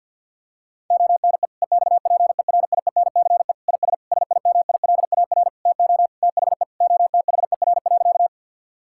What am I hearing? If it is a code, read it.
Morse code, heard as a keyed tone, about 50 wpm, 700 Hz. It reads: ONE EXPERIENCE IS SIMILAR TO THE OTHER.
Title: Sentences from Top 1000 Words - Review Part 5 - 50wpm